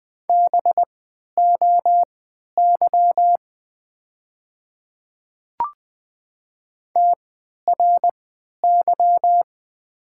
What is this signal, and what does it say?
Morse code, heard as a keyed tone, about 20 wpm, spaced as for 11 wpm, 700 Hz. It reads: BOY TRY
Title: 3x Character Spacing - Top 300 Words - 20wpm